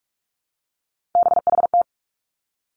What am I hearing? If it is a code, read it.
Morse code, heard as a keyed tone, about 46 wpm, 700 Hz. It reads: BHT
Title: Three Challenging Letter-Number Combinations 45wpm